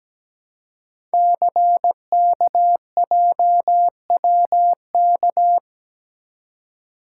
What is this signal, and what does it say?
Morse code, heard as a keyed tone, about 17 wpm, 700 Hz. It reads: CKJWK
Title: Code Groups - 5 Letters - 17wpm